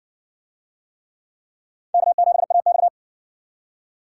Morse code, heard as a keyed tone, about 60 wpm, 700 Hz. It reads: K7AX